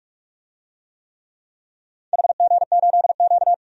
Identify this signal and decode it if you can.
Morse code, heard as a keyed tone, about 46 wpm, 700 Hz. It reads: HG8Q
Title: DXCC-100 Call Signs - 4 Characters - 45wpm